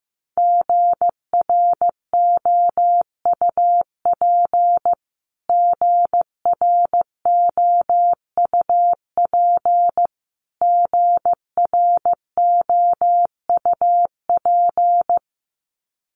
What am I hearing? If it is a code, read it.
Morse code, heard as a keyed tone, about 15 wpm, 700 Hz. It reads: GROUP GROUP GROUP
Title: Top 500 Words Repeated 15wpm